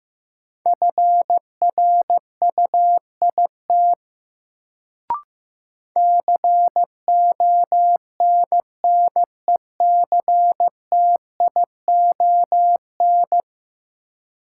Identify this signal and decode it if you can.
Morse code, heard as a keyed tone, about 15 wpm, 700 Hz. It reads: FRUIT CONNECTION
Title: Sentences from Top 1300 Words - New Word Drill Alt - 15wpm